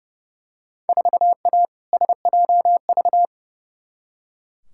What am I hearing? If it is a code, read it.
Morse code, heard as a keyed tone, about 30 wpm, 700 Hz. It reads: 4A SJV
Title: Field Day Exchanges Encore - 30wpm